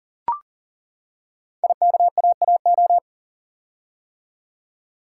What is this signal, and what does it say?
Morse code, heard as a keyed tone, about 39 wpm, 700 Hz. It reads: IKAAO
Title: Code Groups - 5 Letters - 40wpm